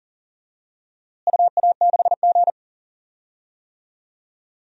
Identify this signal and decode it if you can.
Morse code, heard as a keyed tone, about 40 wpm, 700 Hz. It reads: UA6G
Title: DXCC-100 Call Signs - 4 Characters - Repeated 3x - 40wpm